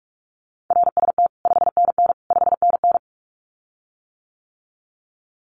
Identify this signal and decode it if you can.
Morse code, heard as a keyed tone, about 44 wpm, 700 Hz. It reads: RST 5NN 5NN
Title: Common QSO Phrases Encore 45wpm